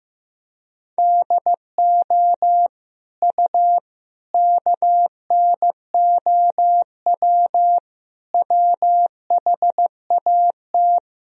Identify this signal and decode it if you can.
Morse code, heard as a keyed tone, about 15 wpm, 700 Hz. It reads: DO U KNOW WHAT
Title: Sentences from Top 100 Words and Abbreviations 15wpm